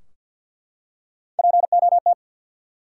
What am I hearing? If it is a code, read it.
Morse code, heard as a keyed tone, about 48 wpm, 700 Hz. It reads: POT
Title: Three Letter Words - 50wpm